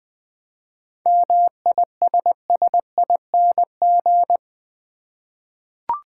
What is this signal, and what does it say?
Morse code, heard as a keyed tone, about 20 wpm, 700 Hz. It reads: MISSING
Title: 7 Letter Words - First 3 Letters Given - 20wpm